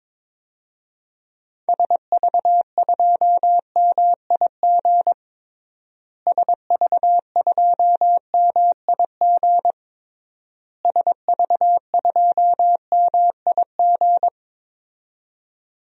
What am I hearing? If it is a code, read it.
Morse code, heard as a keyed tone, about 22 wpm, 700 Hz. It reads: SV2MIG SV2MIG SV2MIG
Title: DXCC-100 Call Signs - 6 Characters - Repeated 3x - 22wpm